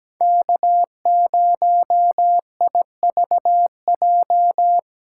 Morse code, heard as a keyed tone, about 20 wpm, 700 Hz. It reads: K0IVJ
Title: Speed Racing - USA 1x3 Call Signs 17wpm